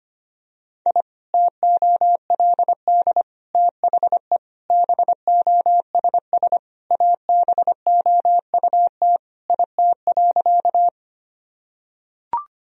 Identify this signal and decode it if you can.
Morse code, heard as a keyed tone, about 25 wpm, 700 Hz. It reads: I TOLD THE BOSS ABOUT IT.
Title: Sentences from Top 1000 Words - Review Part 4 - 25wpm